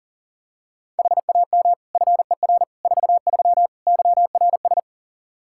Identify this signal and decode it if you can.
Morse code, heard as a keyed tone, about 40 wpm, 700 Hz. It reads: HAM FER 43 YRS
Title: K1USN SST QSO Practice 40wpm